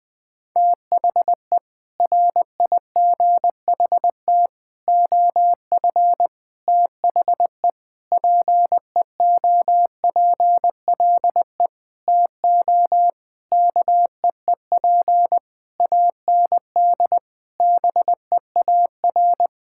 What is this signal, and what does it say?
Morse code, heard as a keyed tone, about 20 wpm, 700 Hz. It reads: THE RIGHT OF THE PEOPLE TO KEEP AND BEAR
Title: Sentences with 9-Word Phrases - Spoken Beginning - 20wpm